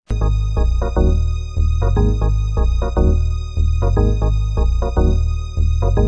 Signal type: none